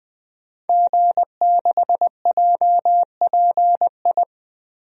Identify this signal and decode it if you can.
Morse code, heard as a keyed tone, about 21 wpm, 700 Hz. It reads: G6JPI
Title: DXCC-100 Call Signs - 5 Characters - Repeated 3x - 20wpm